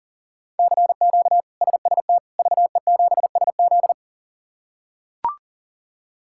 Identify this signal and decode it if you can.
Morse code, heard as a keyed tone, about 40 wpm, 690 Hz. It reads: CQ SST VE7SZ
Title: K1USN SST QSO Practice - 40wpm